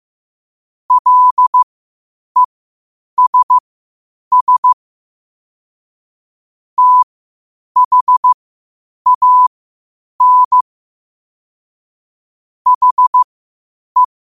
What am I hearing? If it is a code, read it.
Morse code, heard as a keyed tone, about 15 wpm, spaced as for 8 wpm, 1000 Hz. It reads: LESS THAN HE
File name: qotdfiles / 8wpm